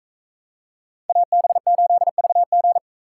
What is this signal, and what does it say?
Morse code, heard as a keyed tone, about 44 wpm, 700 Hz. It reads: AB8VG